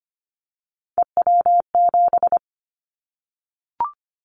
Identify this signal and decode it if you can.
Morse code, heard as a keyed tone, about 25 wpm, 700 Hz. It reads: EW7
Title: Callsign Prefixes for Top 100 DXCC - 25wpm